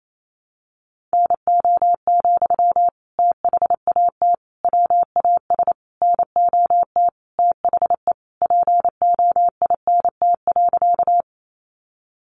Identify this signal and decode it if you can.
Morse code, heard as a keyed tone, about 28 wpm, 700 Hz. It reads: NO, THAT WAS NOT THE POINT.
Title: Sentences from Top 600 Words 28wpm